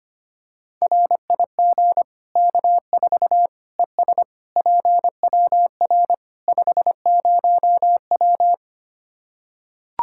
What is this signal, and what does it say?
Morse code, heard as a keyed tone, about 25 wpm, 700 Hz. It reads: RIG K4 ES PWR 50W
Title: K1USN SST QSO Practice 25wpm